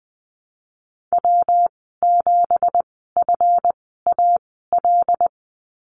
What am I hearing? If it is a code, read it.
Morse code, heard as a keyed tone, about 20 wpm, spaced as for 15 wpm, 700 Hz. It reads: W7FAL
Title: 231 Lesson 37g - Letter x - Call Signs 20wpm